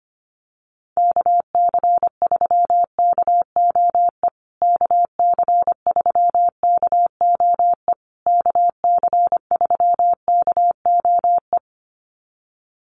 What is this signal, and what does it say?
Morse code, heard as a keyed tone, about 25 wpm, 700 Hz. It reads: KC3KOE KC3KOE KC3KOE
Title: USA Call Signs Repeated 3x 25wpm